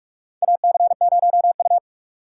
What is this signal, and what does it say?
Morse code, heard as a keyed tone, about 46 wpm, 700 Hz. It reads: AC0U
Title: USA Call Signs - 4 Characters - 6x Character Spacing - 45wpm